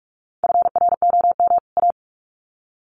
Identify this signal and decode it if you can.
Morse code, heard as a keyed tone, about 44 wpm, 700 Hz. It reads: FROM A
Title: Sets of 2 Words 45wpm